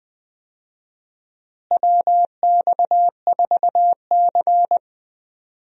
Morse code, heard as a keyed tone, about 20 wpm, 700 Hz. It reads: WX4C
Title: USA Call Signs - 4 Characters - Repeated 2x - 20wpm